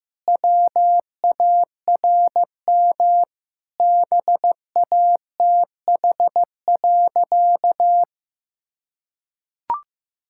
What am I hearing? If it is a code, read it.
Morse code, heard as a keyed tone, about 16 wpm, 700 Hz. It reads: WARM BATH.
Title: Sentences from Top 1600 Words 15wpm